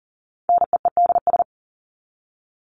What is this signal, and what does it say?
Morse code, heard as a keyed tone, about 40 wpm, 700 Hz. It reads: NEEDS
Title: ES:S Suffix - Top 100 Words with Base Words 40wpm